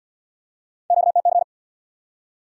Morse code, heard as a keyed tone, about 74 wpm, 700 Hz. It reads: QTC